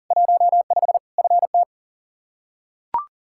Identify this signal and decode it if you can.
Morse code, heard as a keyed tone, about 40 wpm, 700 Hz. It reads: 15 FT
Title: Practice QSO - Exchange 2 - Info - 40wpm